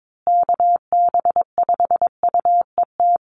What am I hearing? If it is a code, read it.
Morse code, heard as a keyed tone, about 22 wpm, 700 Hz. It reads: KB5UET
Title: USA Call Signs Repeated 3x 22wpm